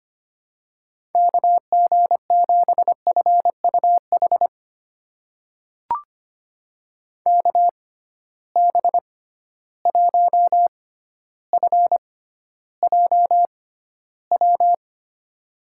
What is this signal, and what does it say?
Morse code, heard as a keyed tone, about 25 wpm, spaced as for 13 wpm, 700 Hz. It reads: KG7FUH KB1FJW
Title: USA Call Signs - 6 Characters - 6x Character Spacing - 25wpm